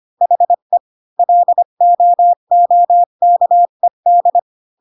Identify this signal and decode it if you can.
Morse code, heard as a keyed tone, about 25 wpm, 700 Hz. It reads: HE LOOKED
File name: Pinocchio / Pinocchio ch1 20 25 ik1hge